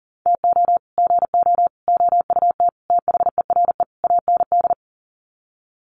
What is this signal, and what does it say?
Morse code, heard as a keyed tone, about 37 wpm, 700 Hz. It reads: TO GO OUT THERE AND
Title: Sets of 5 Words Encore2 40wpm